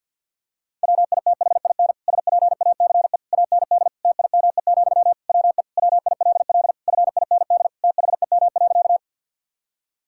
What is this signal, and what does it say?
Morse code, heard as a keyed tone, about 51 wpm, 700 Hz. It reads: WITHIN SPACE AND TIME, WE WILL FIND THEM.
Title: Sentences from Top 1000 Words - Review Part 5 - 50wpm